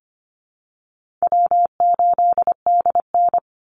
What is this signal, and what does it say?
Morse code, heard as a keyed tone, about 25 wpm, 700 Hz. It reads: W8DN
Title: Call Signs Repeated 2x - 25wpm